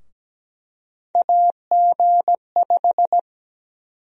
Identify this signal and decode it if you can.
Morse code, heard as a keyed tone, about 17 wpm, 700 Hz. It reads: AG5
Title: USA Call Sign Prefixes - Repeated 2x - 17wpm